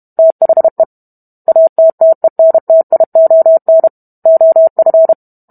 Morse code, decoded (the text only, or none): THE ATTENTION OF